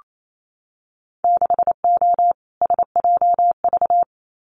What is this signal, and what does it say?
Morse code, heard as a keyed tone, about 28 wpm, 700 Hz. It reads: 6O SJV
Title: Winter Field Day Exchanges Encore 28wpm